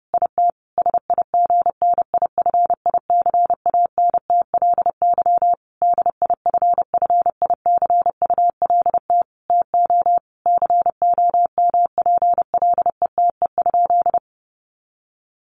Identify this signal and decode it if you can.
Morse code, heard as a keyed tone, about 30 wpm, 700 Hz. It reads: IT SIGNIFICANTLY DIFFICULT TO COMPLETE?
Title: Sentences from Top 1500 Words 30wpm